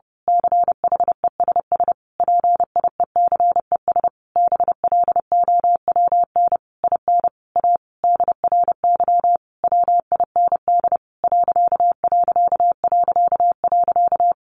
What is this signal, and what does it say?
Morse code, heard as a keyed tone, about 30 wpm, 700 Hz. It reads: CHESS PIECES BLOWN IN A DRY WIND....
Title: A Sound Of Thunder 30wpm